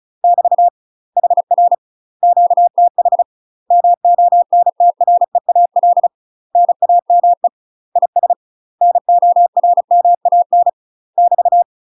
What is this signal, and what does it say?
Morse code, heard as a keyed tone, about 36 wpm, 700 Hz. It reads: <BT> HR QTH MONTREAL NAME IS NORMAN <BT>
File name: UK MASTER QSOs TRAINING/35WPM